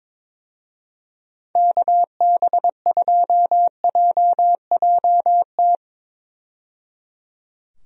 Morse code, heard as a keyed tone, about 21 wpm, 700 Hz. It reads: KB2JJT
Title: USA Call Signs - 6 Characters - 22wpm